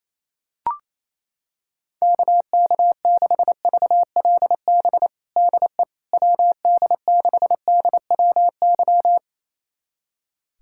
Morse code, heard as a keyed tone, about 28 wpm, 700 Hz. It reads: KK6VLB DE WD6DWY
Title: USA Callsign DE Callsign - 6 Characters - 28wpm